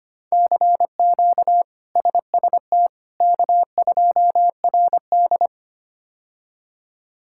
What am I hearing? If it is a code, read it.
Morse code, heard as a keyed tone, about 25 wpm, 700 Hz. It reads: CQ SST K2RD
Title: K1USN SST QSO Practice - 25wpm